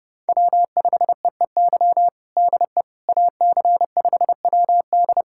Morse code, heard as a keyed tone, about 30 wpm, 700 Hz. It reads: W5EEY DE AC5WD
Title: USA Callsign DE Callsign - 5 Characters - 30wpm